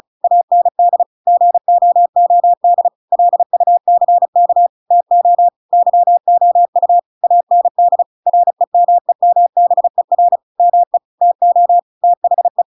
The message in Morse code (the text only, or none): AND GOOD LUCK TO YOU AND REMEMBER ME TO THE